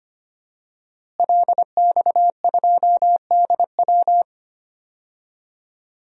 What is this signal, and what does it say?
Morse code, heard as a keyed tone, about 25 wpm, 700 Hz. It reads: LX2DW
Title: DXCC-100 Call Signs - 5 Characters - Repeated 2x - 25wpm